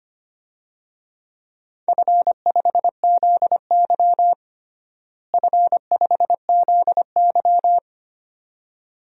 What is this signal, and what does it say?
Morse code, heard as a keyed tone, about 25 wpm, 700 Hz. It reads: F5ZY F5ZY
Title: DXCC-100 Call Signs - 4 Characters - Repeated 2x - 25wpm